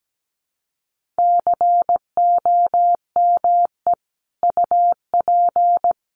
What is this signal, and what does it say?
Morse code, heard as a keyed tone, about 17 wpm, 700 Hz. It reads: COME UP
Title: Sets of 4 Words 17wpm